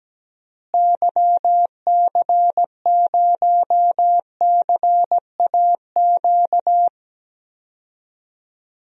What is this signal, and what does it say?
Morse code, heard as a keyed tone, about 17 wpm, 700 Hz. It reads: YC0CAQ
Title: DXCC-100 Call Signs - 6 Characters - Repeated 2x - 17wpm